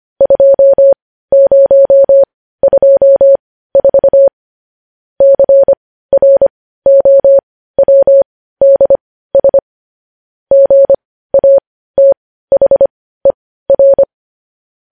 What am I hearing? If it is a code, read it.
Morse code, heard as a keyed tone, about 25 wpm, spaced as for 15 wpm, 550 Hz. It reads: 2024 CROWDS GATHER